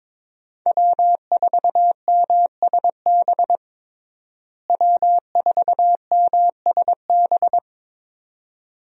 Morse code, heard as a keyed tone, about 22 wpm, 700 Hz. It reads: W4MSB W4MSB
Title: USA Call Signs - 5 Characters - Repeated 2x - 22wpm